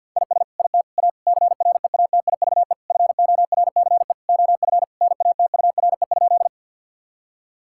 Morse code, heard as a keyed tone, about 50 wpm, 700 Hz. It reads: IS IT A CREATIVE FORCE OF NATURE?
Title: Sentences from Top 1000 Words - Review Part 5 - 50wpm